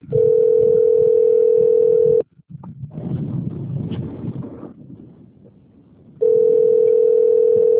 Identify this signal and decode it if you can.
Ringback tone (North America), 440 + 480 Hz.